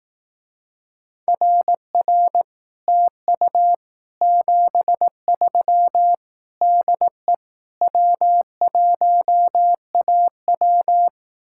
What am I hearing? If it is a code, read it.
Morse code, heard as a keyed tone, about 18 wpm, 700 Hz. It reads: RR TU 73 DE W1AW
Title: Speed-Racing Common QSO Phrases Encore - 15wpm